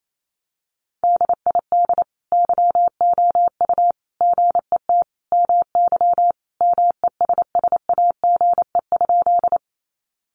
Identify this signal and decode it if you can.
Morse code, heard as a keyed tone, about 28 wpm, 700 Hz. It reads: DID YOU GET MY MESSAGE?
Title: Sentences from Top 1100 Words 28wpm